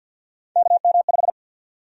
Morse code, heard as a keyed tone, about 50 wpm, 700 Hz. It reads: KM5
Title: USA Call Sign Prefixes - 50wpm